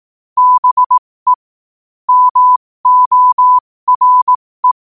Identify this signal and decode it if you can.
Morse code, heard as a keyed tone, about 18 wpm, spaced as for 15 wpm, 1000 Hz. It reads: BE MORE
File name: qotdfiles / 15wpm